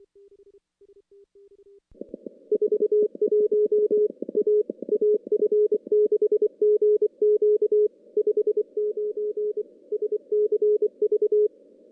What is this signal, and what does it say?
Morse code, heard as a keyed tone, about 24 wpm, 420 Hz. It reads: 41 A AF6GQ 59 SCV
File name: contests / CW-SS-k5tr- -0000